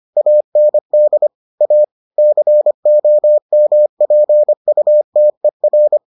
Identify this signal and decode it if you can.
Morse code, heard as a keyed tone, about 25 wpm, 600 Hz. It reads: AND A COMPUTER